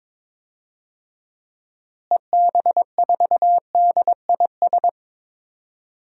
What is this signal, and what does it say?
Morse code, heard as a keyed tone, about 23 wpm, 700 Hz. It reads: EB4DIS
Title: DXCC-100 Call Signs - 6 Characters - Repeated 3x - 22wpm